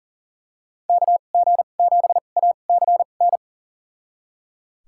Morse code, heard as a keyed tone, about 39 wpm, spaced as for 28 wpm, 690 Hz. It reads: KG7ACN
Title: USA Call Signs - 6 Characters - 2x Character Spacing - 40wpm